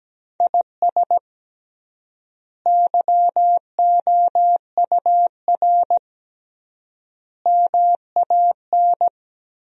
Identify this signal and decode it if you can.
Morse code, heard as a keyed tone, about 17 wpm, 700 Hz. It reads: IS YOUR MAN
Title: 3x Word-Spacing - Sentences from Top 400 Words - 17wpm